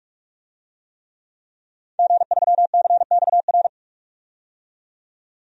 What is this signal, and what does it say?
Morse code, heard as a keyed tone, about 46 wpm, 700 Hz. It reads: G3CXR
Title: DXCC-100 Call Signs - 5 Characters - 45wpm